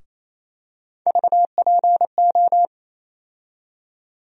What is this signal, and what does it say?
Morse code, heard as a keyed tone, about 28 wpm, 700 Hz. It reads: VPO
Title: USA Call Sign Postfixes - 3 Letters 28wpm